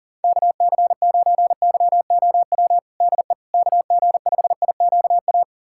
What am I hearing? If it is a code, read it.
Morse code, heard as a keyed tone, about 40 wpm, 700 Hz. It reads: KC9YOW DE KG5IQA
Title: USA Callsign DE Callsign - 6 Characters - 40wpm